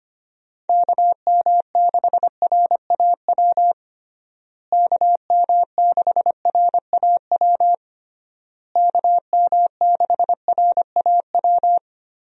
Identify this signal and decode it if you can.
Morse code, heard as a keyed tone, about 25 wpm, 700 Hz. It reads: KM6RAW KM6RAW KM6RAW
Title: USA Call Signs - 6 Characters - Repeated 3x - 25wpm